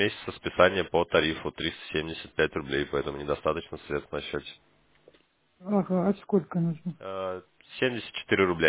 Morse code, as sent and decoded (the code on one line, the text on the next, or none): none